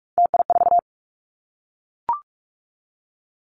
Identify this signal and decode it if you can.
Morse code, heard as a keyed tone, about 44 wpm, 700 Hz. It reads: TI4
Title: Callsign Prefixes for Top 100 DXCC - 45wpm